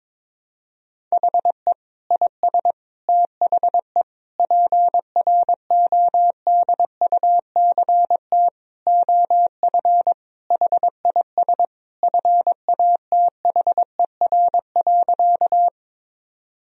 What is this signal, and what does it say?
Morse code, heard as a keyed tone, about 22 wpm, 700 Hz. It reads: HE IS THE PRODUCT OF HIS FATHER.
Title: Sentences from Top 1000 Words - Review Part 2 - 22wpm